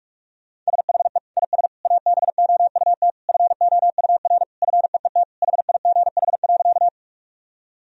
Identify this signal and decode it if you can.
Morse code, heard as a keyed tone, about 46 wpm, 700 Hz. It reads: SHE IS ABOUT FOUR FEET HIGH.
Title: Sentences from Top 1000 Words - Review Part 1 - 45wpm